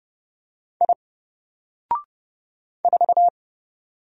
Morse code, heard as a keyed tone, about 30 wpm, 700 Hz. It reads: I 4
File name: Single-Letter-Number-30wpm